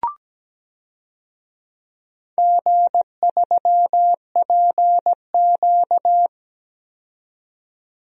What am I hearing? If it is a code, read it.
Morse code, heard as a keyed tone, about 17 wpm, 700 Hz. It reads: G3PQ
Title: DXCC-100 Call Signs - 4 Characters - 17wpm